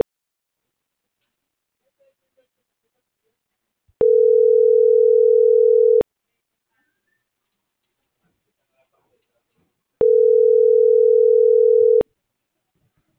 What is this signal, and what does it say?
Ringback tone (North America), 440 + 480 Hz.